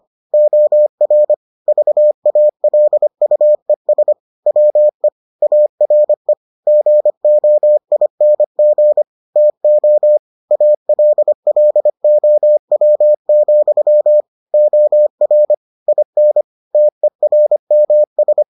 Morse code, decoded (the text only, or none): OR VALUES WE ARE GOING TO ALLOW, OR IN TERMS